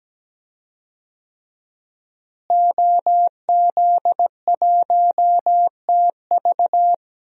Morse code, heard as a keyed tone, about 17 wpm, 700 Hz. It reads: OZ1TV
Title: DXCC-100 Call Signs - 5 Characters - Repeated 3x - 17wpm